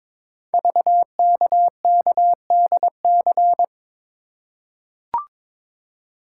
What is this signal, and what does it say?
Morse code, heard as a keyed tone, about 22 wpm, 700 Hz. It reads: VKKDC
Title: Code Groups - 5 Letters - 22wpm